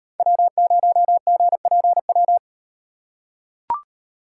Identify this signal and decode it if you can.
Morse code, heard as a keyed tone, about 38 wpm, 700 Hz. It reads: W0GPW